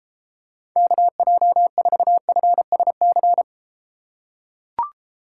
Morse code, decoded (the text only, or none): KJ4FSC